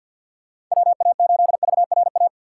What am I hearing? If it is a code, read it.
Morse code, heard as a keyed tone, about 48 wpm, 700 Hz. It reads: WA8VRA
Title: USA Call Signs - 4x Character Spacing - 50wpm